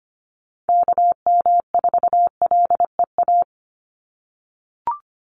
Morse code, heard as a keyed tone, about 25 wpm, 700 Hz. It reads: KM4LEA
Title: USA Call Signs 25wpm